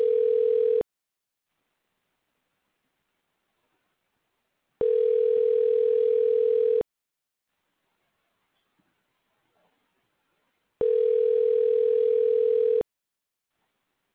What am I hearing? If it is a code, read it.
Ringback tone (North America), 440 + 480 Hz.